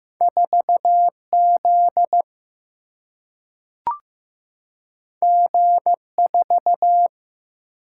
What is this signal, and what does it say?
Morse code, heard as a keyed tone, about 15 wpm, 700 Hz. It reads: 4Z G4
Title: Single-Letter Single-Number 15wpm